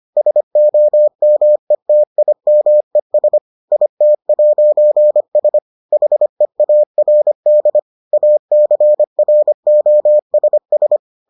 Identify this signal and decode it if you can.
Morse code, heard as a keyed tone, about 25 wpm, 600 Hz. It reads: SOMETIMES IT'S HEARD ACROSS